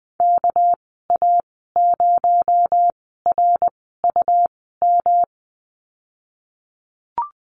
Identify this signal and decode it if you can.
Morse code, heard as a keyed tone, about 20 wpm, spaced as for 15 wpm, 700 Hz. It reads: KA0RUM
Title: 238 Lesson 38g - Letter q - Call Signs 20wpm